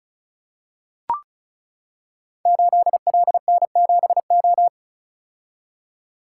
Morse code, heard as a keyed tone, about 36 wpm, 690 Hz. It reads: 8LN7O
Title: Code Groups - 5 Letter:Numbers - 35wpm